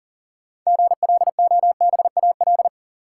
Morse code, heard as a keyed tone, about 40 wpm, 700 Hz. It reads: GLOBAL
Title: Sentences from Top 1200 Words - New Word Drill - 40wpm